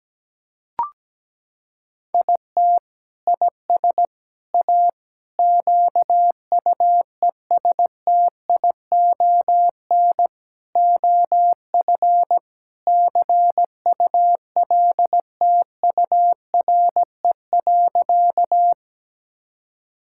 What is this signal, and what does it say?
Morse code, heard as a keyed tone, about 17 wpm, 700 Hz. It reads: IT IS A QUESTION OF CULTURE.
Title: Sentences from Top 1000 Words - Review Part 6 - 17wpm